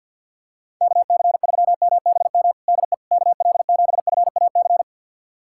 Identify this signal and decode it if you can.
Morse code, heard as a keyed tone, about 53 wpm, 700 Hz. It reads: KC3MBM DE KL7FAC
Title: USA Callsign DE Callsign 50wpm